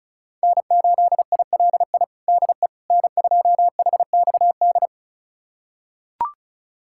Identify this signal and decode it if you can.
Morse code, heard as a keyed tone, about 36 wpm, 700 Hz. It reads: N8ILI DE N2HXD
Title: USA Callsign DE Callsign - 5 Characters - 35wpm